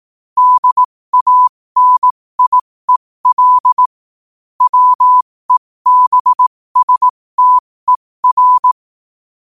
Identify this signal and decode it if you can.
Morse code, heard as a keyed tone, about 18 wpm, spaced as for 15 wpm, 1000 Hz. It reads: DANIEL WEBSTER
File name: qotdfiles / 15wpm